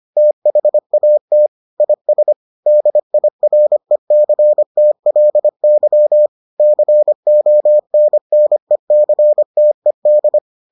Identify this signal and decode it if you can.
Morse code, heard as a keyed tone, about 25 wpm, 600 Hz. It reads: THAT IS DIRECTLY CONNECTED